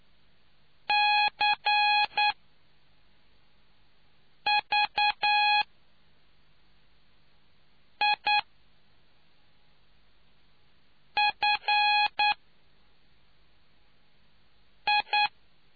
Morse code, heard as a keyed tone, about 9 wpm, spaced as for 3 wpm, 830 Hz. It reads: CVIFI